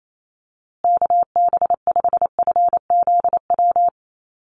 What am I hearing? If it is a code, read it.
Morse code, heard as a keyed tone, about 28 wpm, 700 Hz. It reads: KB5FZW
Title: USA Call Signs Repeated 3x 28wpm